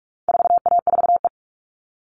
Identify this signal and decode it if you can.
Morse code, heard as a keyed tone, about 44 wpm, 700 Hz. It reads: 4A4E
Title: All Cut Numbers - 4-Digit 45wpm